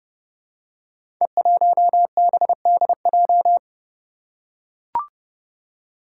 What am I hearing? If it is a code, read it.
Morse code, heard as a keyed tone, about 30 wpm, 700 Hz. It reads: E1BDJ
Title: Code Groups - 5 Letter:Numbers - 30wpm